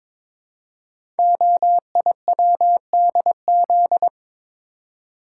Morse code, heard as a keyed tone, about 22 wpm, 700 Hz. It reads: OIWDZ
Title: Code Groups - 5 Letters - 22wpm